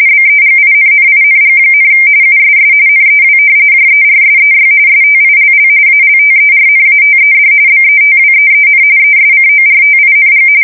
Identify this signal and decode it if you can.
RTTY, about 45.45 baud. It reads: BEGINNING, USS NEW JERSEY STILL ALTERNATED BETWEEN ESCORTING
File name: fsk170 / battleshipnj